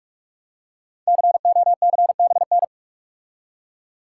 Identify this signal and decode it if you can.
Morse code, heard as a keyed tone, about 44 wpm, 690 Hz. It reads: COCBN
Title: Code Groups - 5 Letters - 45wpm